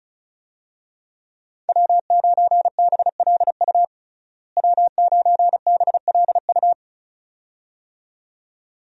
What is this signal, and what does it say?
Morse code, heard as a keyed tone, about 34 wpm, 700 Hz. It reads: W9BLU W9BLU
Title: USA Call Signs - 5 Characters - Repeated 2x - 35wpm